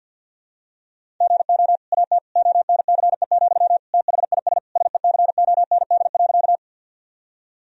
Morse code, heard as a keyed tone, about 52 wpm, 700 Hz. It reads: GO AT ONCE, THIS SECOND.
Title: Sentences from Top 1000 Words - Review Part 1 - 50wpm